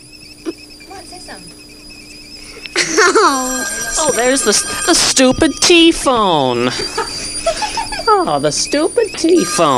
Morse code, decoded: none